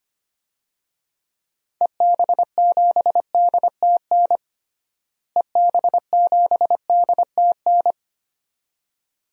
Morse code, heard as a keyed tone, about 25 wpm, 700 Hz. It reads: EB7DTN EB7DTN
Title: DXCC-100 Call Signs - 6 Characters - Repeated 2x - 25wpm